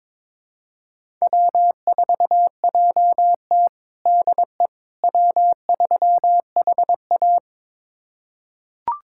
Morse code, heard as a keyed tone, about 22 wpm, 700 Hz. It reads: W4JT DE W3HA
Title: USA Callsign DE Callsign - 4 Characters - 22wpm